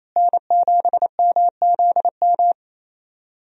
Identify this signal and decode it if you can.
Morse code, heard as a keyed tone, about 27 wpm, 700 Hz. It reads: N7MZM
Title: USA Call Signs - 5 Characters - Repeated 3x - 28wpm